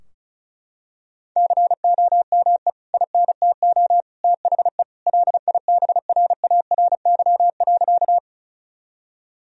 Morse code, heard as a keyed tone, about 36 wpm, 700 Hz. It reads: COME INTO THE LIBRARY.
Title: Sentences from Top 1000 Words - Review Part 4 - 35wpm